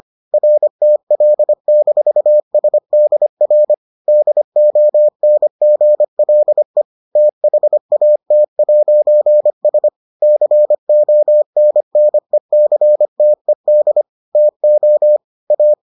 Morse code, heard as a keyed tone, about 25 wpm, 600 Hz. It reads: RTL-SDR DONGLE THAT'S CONNECTED TO A